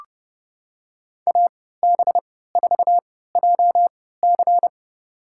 Morse code, heard as a keyed tone, about 30 wpm, spaced as for 17 wpm, 700 Hz. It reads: AB4JC
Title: USA Call Signs - 5 Characters - 3x Character Spacing - 30wpm